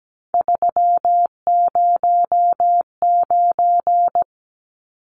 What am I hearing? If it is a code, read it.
Morse code, heard as a keyed tone, about 17 wpm, 700 Hz. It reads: 309
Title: Numbers - 3-Digit 17wpm